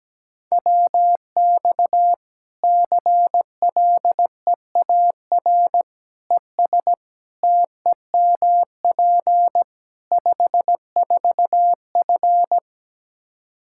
Morse code, heard as a keyed tone, about 17 wpm, 700 Hz. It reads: WX CLEAR ES TEMP 54F
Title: Practice QSO - Exchange 2 - Info - 17wpm